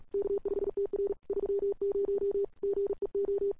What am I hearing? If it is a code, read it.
Morse code, heard as a keyed tone, about 37 wpm, 410 Hz. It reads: K5TR 30 GEO
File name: CWSprint / Feb2007 / 2007-feb-cw-sprint-k5tr-op-n5rz -00:00